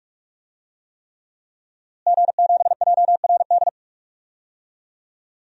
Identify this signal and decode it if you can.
Morse code, heard as a keyed tone, about 44 wpm, 700 Hz. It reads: G7JRD
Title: DXCC-100 Call Signs - 5 Characters - Repeated 3x - 45wpm